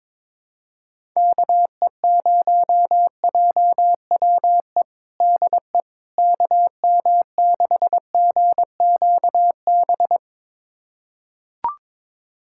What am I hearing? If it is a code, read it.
Morse code, heard as a keyed tone, about 22 wpm, 700 Hz. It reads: KE0JWE DE KM6GQB
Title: USA Callsign DE Callsign 22wpm